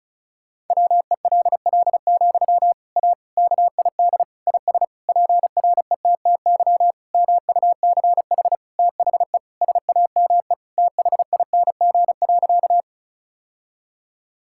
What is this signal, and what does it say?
Morse code, heard as a keyed tone, about 34 wpm, 700 Hz. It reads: WELL, A KID IS PRETTY MUCH THE SAME THING.
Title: Sentences from Top 1600 Words 35wpm